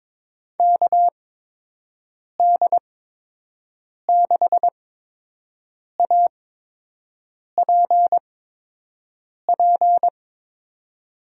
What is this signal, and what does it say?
Morse code, heard as a keyed tone, about 22 wpm, spaced as for 6 wpm, 700 Hz. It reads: KD6APP
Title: USA Call Signs - 6 Characters - 8x Character Spacing - 22wpm